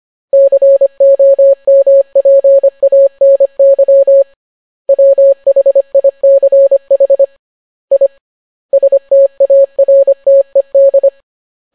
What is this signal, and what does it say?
Morse code, heard as a keyed tone, about 25 wpm, 550 Hz. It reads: COMPANY WHICH I STARTED